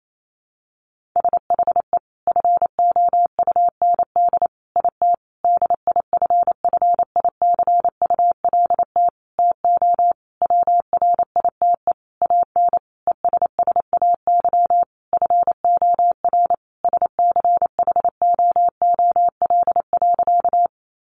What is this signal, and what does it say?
Morse code, heard as a keyed tone, about 28 wpm, 700 Hz. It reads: SHE FOUND IT DIFFICULT TO WRITE AN ESSAY FOR SCHOOL.
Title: Sentences from Top 1500 Words 28wpm